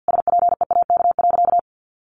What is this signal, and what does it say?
Morse code, heard as a keyed tone, about 50 wpm, 700 Hz. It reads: SPEAK.
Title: Sentences from Top 1500 Words 50wpm